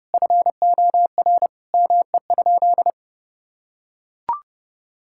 Morse code, heard as a keyed tone, about 30 wpm, 700 Hz. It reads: FOR ME?
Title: Sentences from Top 1600 Words 30wpm